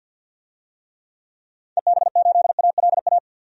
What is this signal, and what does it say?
Morse code, heard as a keyed tone, about 48 wpm, 700 Hz. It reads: EB8ALA
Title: DXCC-100 Call Signs - 6 Characters - 50wpm